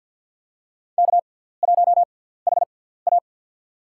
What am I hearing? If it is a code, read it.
Morse code, heard as a keyed tone, about 53 wpm, spaced as for 17 wpm, 700 Hz. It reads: K1HA
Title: USA Call Signs - 4 Characters - 6x Character Spacing - 50wpm